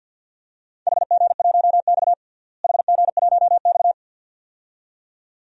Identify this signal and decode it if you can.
Morse code, heard as a keyed tone, about 52 wpm, 700 Hz. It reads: HG1X HG1X
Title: DXCC-100 Call Signs - 4 Characters - Repeated 2x - 50wpm